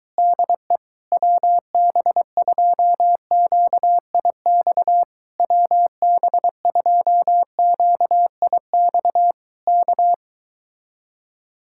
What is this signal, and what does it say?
Morse code, heard as a keyed tone, about 23 wpm, 700 Hz. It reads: DE WB2QIX WB2QIX K